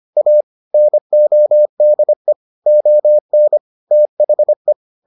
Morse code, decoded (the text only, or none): A NODE ON THE